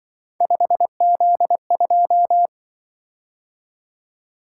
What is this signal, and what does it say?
Morse code, heard as a keyed tone, about 23 wpm, 700 Hz. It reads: 5Z2
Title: Speed-Racing - Three Letter-Number Combinations - 20wpm